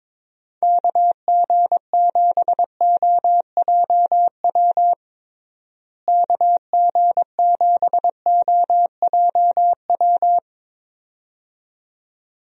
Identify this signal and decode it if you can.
Morse code, heard as a keyed tone, about 22 wpm, 700 Hz. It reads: KG7OJW KG7OJW
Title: USA Call Signs - 6 Characters - Repeated 2x - 22wpm